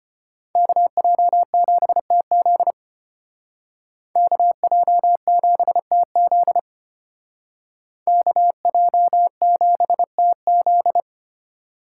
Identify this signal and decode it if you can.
Morse code, heard as a keyed tone, about 31 wpm, 700 Hz. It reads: KJ7TZ KJ7TZ KJ7TZ
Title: Speed Racing - USA 2x2 Call Signs 25wpm